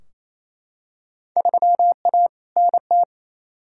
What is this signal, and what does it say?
Morse code, heard as a keyed tone, about 28 wpm, 700 Hz. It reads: 3A NT
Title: Field Day Exchanges Encore - 28wpm